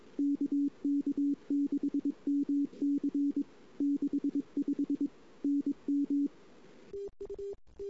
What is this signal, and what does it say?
Morse code, heard as a keyed tone, about 22 wpm, 280 Hz. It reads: KK6MC 65 NM
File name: CW / 2008-ss-cw-k5tr-op-n5rz -08:00